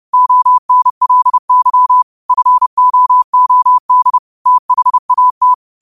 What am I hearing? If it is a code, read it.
Morse code, heard as a keyed tone, about 30 wpm, 1000 Hz. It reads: ONLY FOOD THAT
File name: qotdfiles / 30wpm